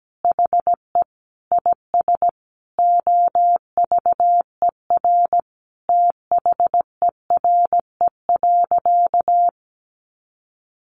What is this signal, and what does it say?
Morse code, heard as a keyed tone, about 17 wpm, 700 Hz. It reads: HE IS OVER THERE.
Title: Sentences from Top 600 Words 17wpm